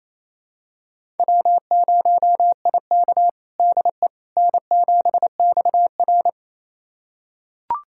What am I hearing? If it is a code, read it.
Morse code, heard as a keyed tone, about 28 wpm, 700 Hz. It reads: W0IK DE N7XR
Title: USA Callsign DE Callsign - 4 Characters - 28wpm